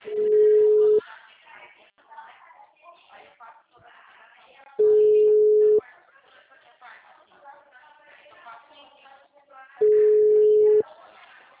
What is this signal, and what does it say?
Ringback tone (Europe, 425 Hz), with speech.